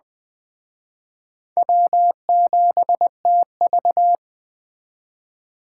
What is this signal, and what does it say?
Morse code, heard as a keyed tone, about 21 wpm, 700 Hz. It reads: W7TV